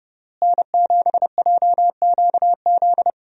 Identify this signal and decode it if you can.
Morse code, heard as a keyed tone, about 30 wpm, 700 Hz. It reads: N7JQZ
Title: USA Call Signs - 5 Characters - Repeated 3x - 30wpm